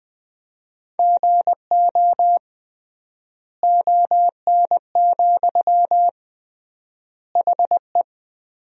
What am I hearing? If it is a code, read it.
Morse code, heard as a keyed tone, about 20 wpm, 700 Hz. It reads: GO ON, HE
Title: 3x Word-Spacing - Sentences from Top 100 Words - 20wpm